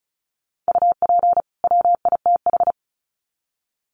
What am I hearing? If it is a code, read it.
Morse code, heard as a keyed tone, about 34 wpm, 700 Hz. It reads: UP WITH
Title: Sets of 2 Words 35wpm